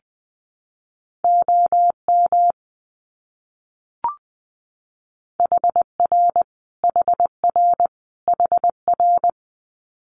Morse code, heard as a keyed tone, about 19 wpm, 700 Hz. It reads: OM HR HR HR
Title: Abbreviations Repeated 3x - 20wpm